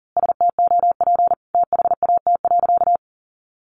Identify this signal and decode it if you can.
Morse code, heard as a keyed tone, about 40 wpm, 700 Hz. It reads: STOP THAT.
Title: A Sound Of Thunder 40wpm